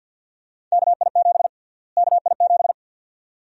Morse code, heard as a keyed tone, about 52 wpm, 700 Hz. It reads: KI7 KI7
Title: USA Call Sign Prefixes - Repeated 2x - 50wpm